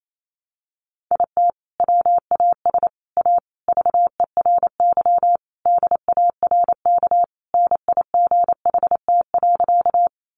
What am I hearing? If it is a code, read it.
Morse code, heard as a keyed tone, about 28 wpm, 700 Hz. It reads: IT WAS A VERY DARK NIGHT.
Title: Sentences from Top 500 Words 28wpm